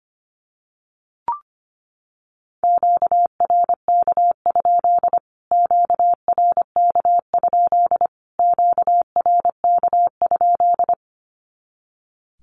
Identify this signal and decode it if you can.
Morse code, heard as a keyed tone, about 25 wpm, 700 Hz. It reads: QRK? QRK? QRK?
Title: Q-Codes Repeated 3x 25wpm